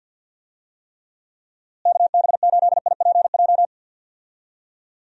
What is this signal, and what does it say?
Morse code, heard as a keyed tone, about 50 wpm, 700 Hz. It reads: KB8IPJ
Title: USA Call Signs - 6 Characters - Repeated 3x - 50wpm